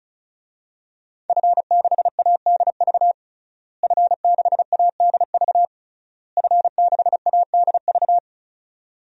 Morse code, heard as a keyed tone, about 36 wpm, 700 Hz. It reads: F6ADV F6ADV F6ADV
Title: DXCC-100 Call Signs - 5 Characters - Repeated 3x - 35wpm